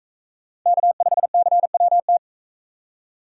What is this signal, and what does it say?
Morse code, heard as a keyed tone, about 43 wpm, 700 Hz. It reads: K5CWT